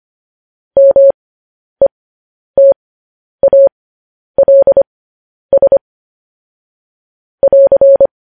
Morse code, heard as a keyed tone, about 25 wpm, spaced as for 10 wpm, 550 Hz. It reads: METALS <AR>
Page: News Headlines in Morse Code at 10 WPM